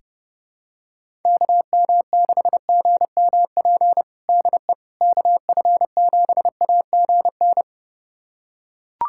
Morse code, heard as a keyed tone, about 32 wpm, 700 Hz. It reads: KM6GMP DE KF7AGN
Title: USA Callsign DE Callsign - 6 Characters - 30wpm